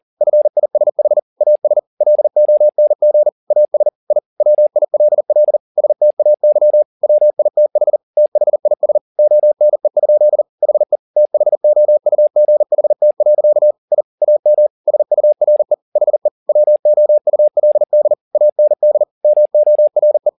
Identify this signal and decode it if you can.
Morse code, heard as a keyed tone, about 40 wpm, 600 Hz. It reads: FISH AS LONG AS I WILL STAY WITH THIS ONE? HE THOUGHT. I AM SURE HE WOULD AND MORE